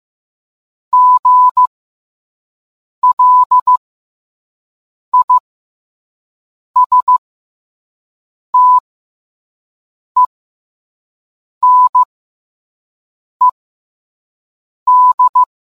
Morse code, heard as a keyed tone, about 15 wpm, spaced as for 5 wpm, 1000 Hz. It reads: GLISTENED